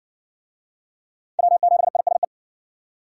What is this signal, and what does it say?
Morse code, heard as a keyed tone, about 58 wpm, 700 Hz. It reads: W7ISE